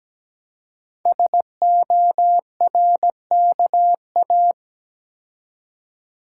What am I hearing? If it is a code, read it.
Morse code, heard as a keyed tone, about 17 wpm, 700 Hz. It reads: SORKA
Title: Code Groups - 5 Letters - 17wpm